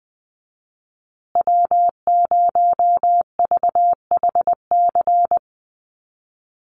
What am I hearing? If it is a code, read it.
Morse code, heard as a keyed tone, about 20 wpm, 700 Hz. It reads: W0VHC
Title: USA Call Signs Repeated 3x 20wpm